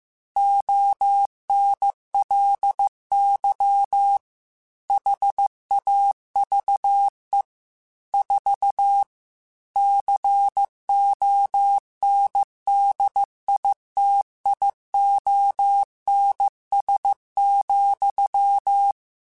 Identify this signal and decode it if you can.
Morse code, heard as a keyed tone, about 15 wpm, 800 Hz. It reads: ONLY HAVE 4 CONDITIONS,
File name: Technical Subjects / Technical Subjects-mp3s / AND Gate @13